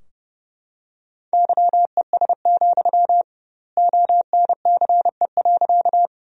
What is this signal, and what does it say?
Morse code, heard as a keyed tone, about 32 wpm, 700 Hz. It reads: YES, ONCE.
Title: 2x Word-Spacing - Sentences from Top 500 Words - 30wpm